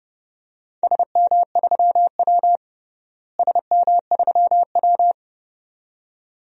Morse code, heard as a keyed tone, about 32 wpm, 700 Hz. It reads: SM3W SM3W
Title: DXCC-100 Call Signs - 4 Characters - Repeated 2x - 30wpm